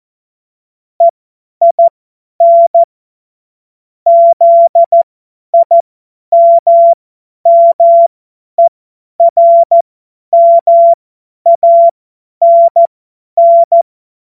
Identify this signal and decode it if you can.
Morse code, heard as a keyed tone, about 14 wpm, spaced as for 10 wpm, 680 Hz. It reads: EIN ZIMMERMANN
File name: Morsekurs / Witze / Witz 17.Oktober 2023 50 F70